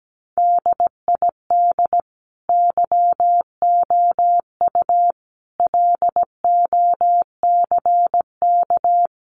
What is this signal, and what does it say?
Morse code, heard as a keyed tone, about 17 wpm, 700 Hz. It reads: DID YOU LOCK
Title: Sentences from Top 1300 Words 17wpm